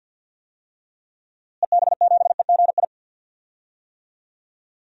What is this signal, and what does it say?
Morse code, heard as a keyed tone, about 52 wpm, 700 Hz. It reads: EB7EGI
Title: DXCC-100 Call Signs - 6 Characters - Repeated 3x - 50wpm